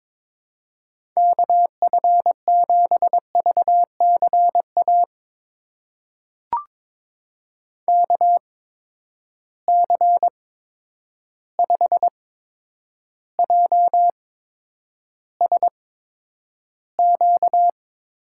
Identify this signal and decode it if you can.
Morse code, heard as a keyed tone, about 22 wpm, spaced as for 9 wpm, 700 Hz. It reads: KF7VCA KC5JSQ
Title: USA Call Signs - 8x Character Spacing - 22wpm